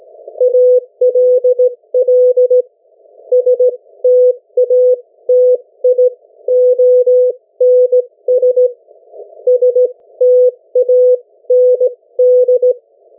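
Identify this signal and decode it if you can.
Morse code, heard as a keyed tone, about 17 wpm, spaced as for 14 wpm, 500 Hz. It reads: ALL STATIONS STAND
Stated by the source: Morse code from the national amateur radio CW net of New Zealand, which meets on 3535.0 kHz or 7030 kHz.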